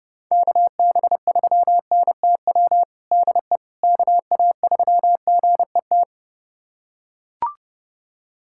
Morse code, heard as a keyed tone, about 30 wpm, 700 Hz. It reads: KB3NTW DE KA3GET
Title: USA Callsign DE Callsign - 6 Characters - 30wpm